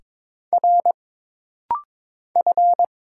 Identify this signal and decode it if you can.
Morse code, heard as a keyed tone, about 22 wpm, 700 Hz. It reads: R F
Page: This rapid-fire timing is geared to push your ICR (Instant Character Recognition) skills.